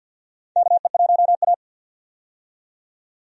Morse code, heard as a keyed tone, about 49 wpm, 700 Hz. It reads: KE1A
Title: USA Call Signs - 4 Characters - 50wpm